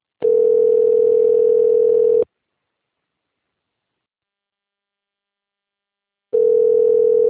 Ringback tone (North America), 440 + 480 Hz.